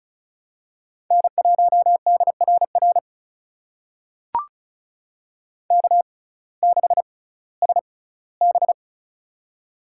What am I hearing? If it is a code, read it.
Morse code, heard as a keyed tone, about 36 wpm, spaced as for 19 wpm, 700 Hz. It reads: N1DRR K6SB
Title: USA Call Signs - 6x Character Spacing - 35wpm